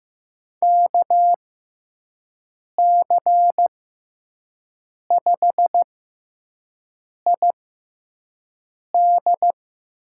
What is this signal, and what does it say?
Morse code, heard as a keyed tone, about 15 wpm, spaced as for 5 wpm, 700 Hz. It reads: KC5ID
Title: USA Call Signs - 6x Character Spacing - 15wpm